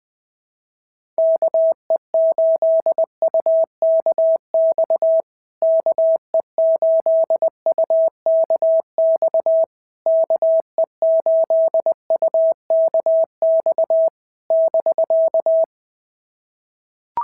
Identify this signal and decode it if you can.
Morse code, heard as a keyed tone, about 20 wpm, 650 Hz. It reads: KE8UKX KE8UKX KE8UKX <BK>
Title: POTA QSO Practice - 20wpm